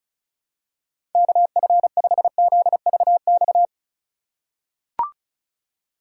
Morse code, heard as a keyed tone, about 36 wpm, 700 Hz. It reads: KF5ZVX
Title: USA Call Signs - 6 Characters - 3x Character Spacing - 35wpm